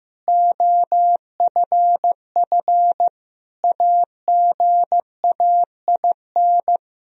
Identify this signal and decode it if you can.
Morse code, heard as a keyed tone, about 15 wpm, 700 Hz. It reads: OFF AGAIN
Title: Sentences from Top 1000 Words - Review Part 3 - 15wpm